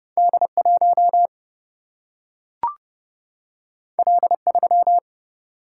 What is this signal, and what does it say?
Morse code, heard as a keyed tone, about 31 wpm, 700 Hz. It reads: D1 L3
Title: Single-Letter Single-Number 30wpm